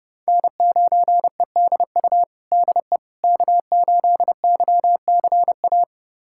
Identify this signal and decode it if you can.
Morse code, heard as a keyed tone, about 30 wpm, 700 Hz. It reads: N9EDU DE K8YCA
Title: USA Callsign DE Callsign - 5 Characters - 30wpm